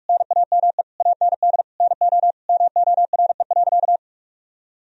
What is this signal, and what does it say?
Morse code, heard as a keyed tone, about 41 wpm, 700 Hz. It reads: NAME AND NO MORE.
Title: Sentences from Top 1000 Words - Review Part 1 - 45wpm